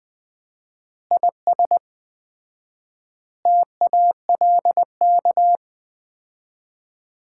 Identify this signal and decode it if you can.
Morse code, heard as a keyed tone, about 20 wpm, 700 Hz. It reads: IS TALK
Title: 4x Word-Spacing - Sets of 5 Words - 20wpm